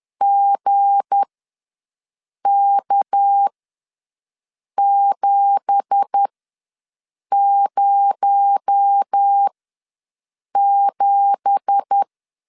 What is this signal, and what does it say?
Morse code, heard as a keyed tone, about 10 wpm, spaced as for 5 wpm, 800 Hz. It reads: GK707
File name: LEZIONI / 9 2867 ws 10wpm 4215